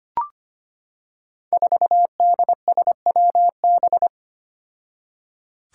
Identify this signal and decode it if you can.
Morse code, heard as a keyed tone, about 25 wpm, 700 Hz. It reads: 4DSWB
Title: Five Challenging Letter-Number Combinations 25wpm